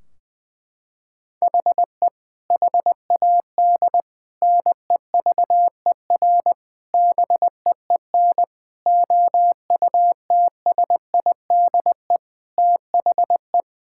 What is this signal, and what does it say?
Morse code, heard as a keyed tone, about 20 wpm, 700 Hz. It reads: HE HAD NEVER BEEN OUTSIDE THE
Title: Sentences from Top 1600 Words 20wpm